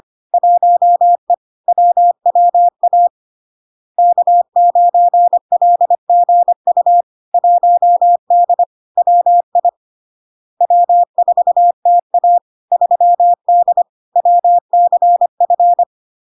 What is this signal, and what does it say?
Morse code, heard as a keyed tone, about 25 wpm, 700 Hz. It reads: 1E WWA K9LGU 1D WI W4TA 3D WCF
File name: LARK-FD-2022-Log-in-CW-25wpm-700-Hz